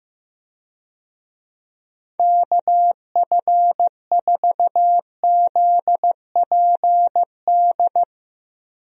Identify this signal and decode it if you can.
Morse code, heard as a keyed tone, about 15 wpm, 700 Hz. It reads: KF4ZPD
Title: USA Call Signs - 6 Characters - 15wpm